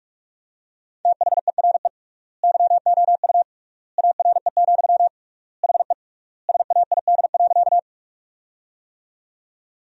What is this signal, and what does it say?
Morse code, heard as a keyed tone, about 44 wpm, 700 Hz. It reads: THERE YOU ARE, HE SAID.
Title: 3x Word-Spacing - Sentences from Top 100 Words - 45wpm